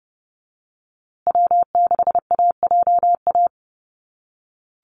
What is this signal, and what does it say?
Morse code, heard as a keyed tone, about 30 wpm, 700 Hz. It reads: W6AJA
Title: USA Call Signs Repeated 2x 30wpm